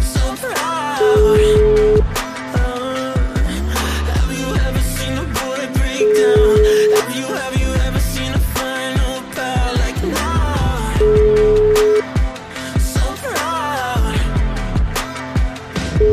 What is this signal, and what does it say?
Ringback tone (Europe, 425 Hz), with noise.